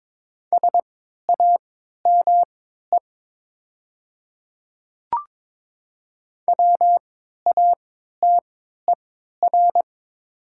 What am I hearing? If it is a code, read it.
Morse code, heard as a keyed tone, about 23 wpm, spaced as for 13 wpm, 700 Hz. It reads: SAME WATER
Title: 3x Character Spacing - Top 300 Words - 22wpm